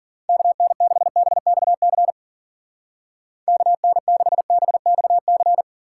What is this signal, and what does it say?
Morse code, heard as a keyed tone, about 50 wpm, 700 Hz. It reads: KN6BXC KN6BXC
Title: Speed Racing - USA 2x3 Call Signs 40wpm